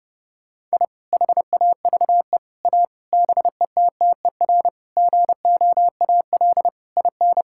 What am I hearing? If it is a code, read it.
Morse code, heard as a keyed tone, about 30 wpm, 700 Hz. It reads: I HAVE A BETTER GOAL IN
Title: Sentences from Top 1000 Words - Review Part 5 - 30wpm